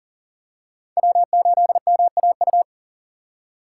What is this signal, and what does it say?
Morse code, heard as a keyed tone, about 40 wpm, 700 Hz. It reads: W8MAU
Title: USA Call Signs - 3x Character Spacing - 40wpm